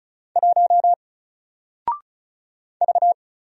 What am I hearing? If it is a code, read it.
Morse code, heard as a keyed tone, about 35 wpm, 700 Hz. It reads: 1 V
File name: Single-Letter-Number-35wpm